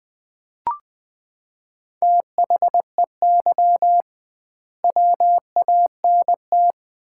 Morse code, heard as a keyed tone, about 20 wpm, 700 Hz. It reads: THEY WANT
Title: 2x Word-Spacing - Sets of 3 Words - 20wpm